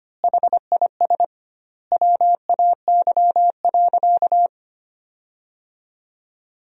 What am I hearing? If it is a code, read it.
Morse code, heard as a keyed tone, about 25 wpm, 700 Hz. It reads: HIS WAY.
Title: 2x Word-Spacing - Sentences from Top 300 Words - 25wpm